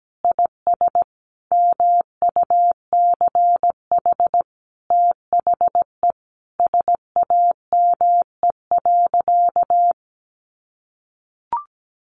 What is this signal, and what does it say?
Morse code, heard as a keyed tone, about 17 wpm, 700 Hz. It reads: IS MUCH THE SAME.
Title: Sentences from Top 1200 Words 17wpm